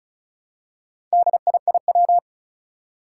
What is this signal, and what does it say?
Morse code, heard as a keyed tone, about 36 wpm, 700 Hz. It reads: DIIW
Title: Four Challenging Letter-Number Combinations 35wpm